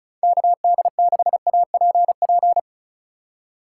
Morse code, heard as a keyed tone, about 36 wpm, 700 Hz. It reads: KD6APP
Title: USA Call Signs - 6 Characters - 4x Character Spacing - 35wpm